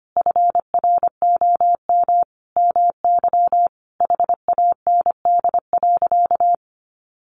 Morse code, heard as a keyed tone, about 25 wpm, 700 Hz. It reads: FROM MY HAND.
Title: Sentences from Top 1300 Words 25wpm